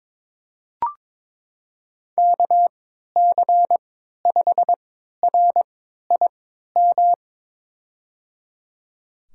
Morse code, heard as a keyed tone, about 23 wpm, spaced as for 13 wpm, 700 Hz. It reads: KC5RIM
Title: USA Call Signs - 3x Character Spacing - 22wpm